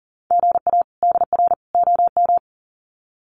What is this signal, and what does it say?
Morse code, heard as a keyed tone, about 40 wpm, 700 Hz. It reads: GA DR OM
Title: Common QSO Phrases 40wpm